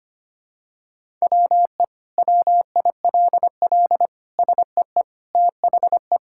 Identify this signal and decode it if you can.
Morse code, heard as a keyed tone, about 25 wpm, 700 Hz. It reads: WE WILL SEE THE
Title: Sentences from Top 1000 Words - Review Part 5 - 25wpm